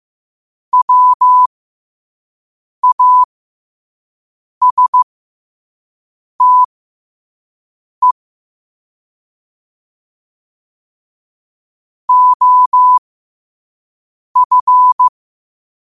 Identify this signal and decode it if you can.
Morse code, heard as a keyed tone, about 15 wpm, spaced as for 5 wpm, 1000 Hz. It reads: WASTE OF
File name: 2026-02-15-5wpm